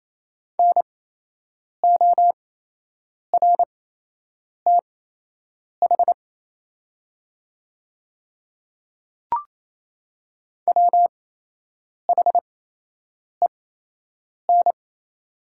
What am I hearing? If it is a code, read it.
Morse code, heard as a keyed tone, about 28 wpm, spaced as for 8 wpm, 700 Hz. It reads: NORTH WHEN
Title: 8x Character Spacing - Top 300 Words - 28wpm